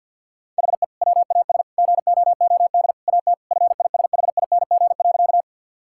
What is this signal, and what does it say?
Morse code, heard as a keyed tone, about 50 wpm, 700 Hz. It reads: HE WAS GOOD AT FISHING.
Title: Sentences from Top 1000 Words - Review Part 5 - 50wpm